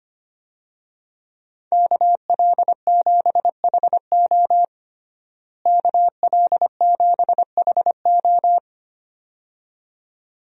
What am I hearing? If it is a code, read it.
Morse code, heard as a keyed tone, about 25 wpm, 700 Hz. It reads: KL7HO KL7HO
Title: DXCC-100 Call Signs - 5 Characters - Repeated 2x - 25wpm